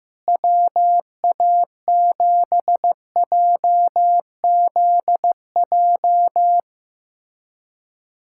Morse code, heard as a keyed tone, about 15 wpm, 700 Hz. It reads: WA7JZJ
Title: USA Callsign DE Callsign - 6 Characters - 15wpm